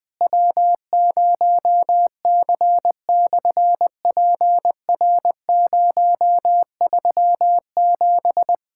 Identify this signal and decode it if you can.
Morse code, heard as a keyed tone, about 20 wpm, 700 Hz. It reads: W0C/PR037
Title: SOTA QSO Practice 20wpm